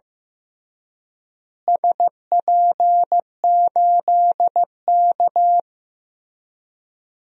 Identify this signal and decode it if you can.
Morse code, heard as a keyed tone, about 15 wpm, 700 Hz. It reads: SP8K
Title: DXCC-100 Call Signs - 4 Characters - Repeated 3x - 15wpm